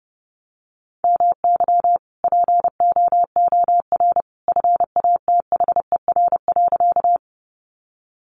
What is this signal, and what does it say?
Morse code, heard as a keyed tone, about 30 wpm, 700 Hz. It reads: MY POOR FATHER.
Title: Sentences from Top 1000 Words 30wpm